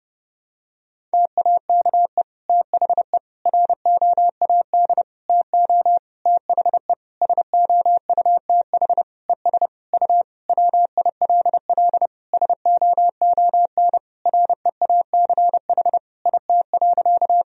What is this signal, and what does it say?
Morse code, heard as a keyed tone, about 32 wpm, 700 Hz. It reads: TAKE THE ROAD TO THE SOUTH ES U WILL SOON REACH IT.
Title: Sentences from Top 400 Words and Abbreviations 30wpm